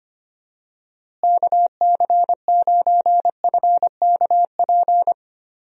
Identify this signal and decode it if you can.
Morse code, heard as a keyed tone, about 25 wpm, 700 Hz. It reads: KC9FKP
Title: USA Call Signs - 6 Characters - Repeated 2x - 25wpm